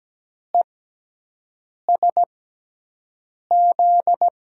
Morse code, heard as a keyed tone, about 17 wpm, spaced as for 6 wpm, 700 Hz. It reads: ESZ
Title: USA Call Signs - 6 Characters - 6x Character Spacing - 17wpm